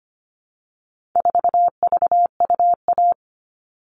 Morse code, heard as a keyed tone, about 25 wpm, 700 Hz. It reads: 4VUA
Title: All Cut Numbers - 4-Digit 25wpm